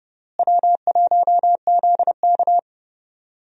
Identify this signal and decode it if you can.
Morse code, heard as a keyed tone, about 30 wpm, 700 Hz. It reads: W1ZK
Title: USA Call Signs - 4 Characters - Repeated 3x - 30wpm